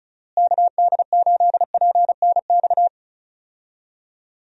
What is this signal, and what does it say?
Morse code, heard as a keyed tone, about 34 wpm, 700 Hz. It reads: KD8PNX
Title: USA Call Signs - 6 Characters - Repeated 3x - 35wpm